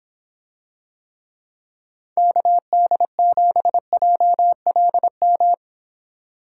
Morse code, heard as a keyed tone, about 26 wpm, 700 Hz. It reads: KD7JLM